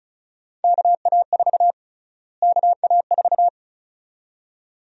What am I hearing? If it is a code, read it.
Morse code, heard as a keyed tone, about 34 wpm, 700 Hz. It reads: KA4 KA4
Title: USA Call Sign Prefixes - Repeated 2x - 35wpm